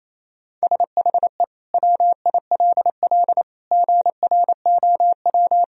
Morse code, heard as a keyed tone, about 28 wpm, 700 Hz. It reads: SHE WILL GROW
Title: Sentences from Top 1000 Words - Review Part 1 - 28wpm